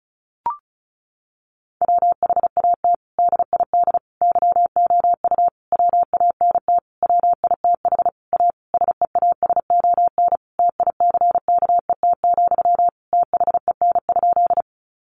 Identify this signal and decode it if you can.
Morse code, heard as a keyed tone, about 34 wpm, 700 Hz. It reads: WHAT DID YOU WANT WITH A SEASON TICKET, THEN?
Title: Sentences from Top 1400 Words 35wpm